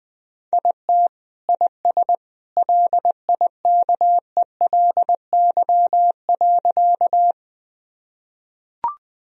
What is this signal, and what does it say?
Morse code, heard as a keyed tone, about 20 wpm, 700 Hz. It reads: IT IS LIKELY.
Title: Sentences from Top 1000 Words - Review Part 4 - 20wpm